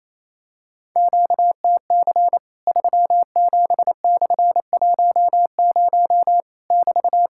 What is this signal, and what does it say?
Morse code, heard as a keyed tone, about 28 wpm, 700 Hz. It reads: QTC 37/10 <BT>
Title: Worked All Europe DX Contest - QTC Exchange - 28wpm